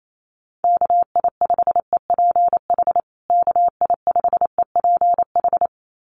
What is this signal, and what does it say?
Morse code, heard as a keyed tone, about 28 wpm, 700 Hz. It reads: KI5EPH KI5EPH
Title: USA Call Signs Repeated 2x 28wpm